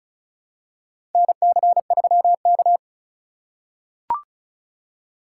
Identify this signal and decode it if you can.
Morse code, heard as a keyed tone, about 36 wpm, 700 Hz. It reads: NC3K
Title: USA Call Signs - 4 Characters - 3x Character Spacing - 35wpm